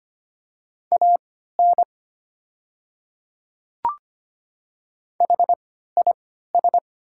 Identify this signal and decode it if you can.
Morse code, heard as a keyed tone, about 25 wpm, spaced as for 14 wpm, 700 Hz. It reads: AN HIS
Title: 3x Character Spacing - Top 200 Words - 25wpm